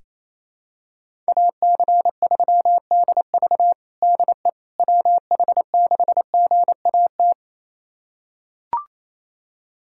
Morse code, heard as a keyed tone, about 28 wpm, 700 Hz. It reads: AC3DV DE WH6GAT
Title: USA Callsign DE Callsign 28wpm